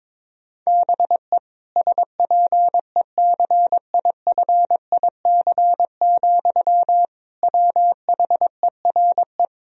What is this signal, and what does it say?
Morse code, heard as a keyed tone, about 23 wpm, 700 Hz. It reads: BE SPECIFIC, WHERE
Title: Sentences from Top 1000 Words - Review Part 3 - 22wpm